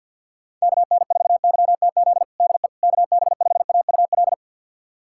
Morse code, heard as a keyed tone, about 51 wpm, 700 Hz. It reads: KN4YTZ DE KB5AUL
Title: USA Callsign DE Callsign - 6 Characters - 50wpm